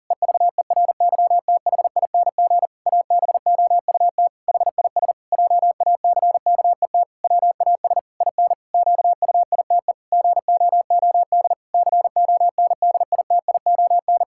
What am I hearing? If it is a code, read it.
Morse code, heard as a keyed tone, about 37 wpm, 700 Hz. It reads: EVERYTHING ABOUT HIS JACKET WAS IN QUITE GOOD CONDITION
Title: Pangram Challenge v2 - 40wpm